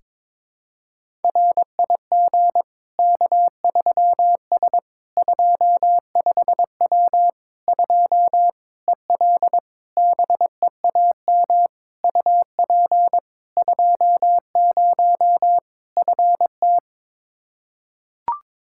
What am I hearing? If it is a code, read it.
Morse code, heard as a keyed tone, about 22 wpm, 700 Hz. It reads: RIG K3S 25W 2 EL BEAM UP 20 FT
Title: Practice QSO - Exchange 2 - Info - 22wpm